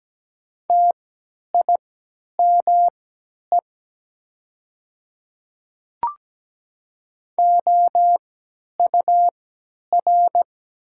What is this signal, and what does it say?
Morse code, heard as a keyed tone, about 17 wpm, spaced as for 10 wpm, 700 Hz. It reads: TIME OUR
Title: 3x Character Spacing - Top 200 Words - 17wpm